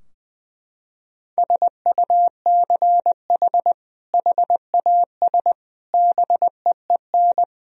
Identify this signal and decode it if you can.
Morse code, heard as a keyed tone, about 20 wpm, 700 Hz. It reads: SUCH HAS BEEN
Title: Sentences from Top 1000 Words - Review Part 3 - 20wpm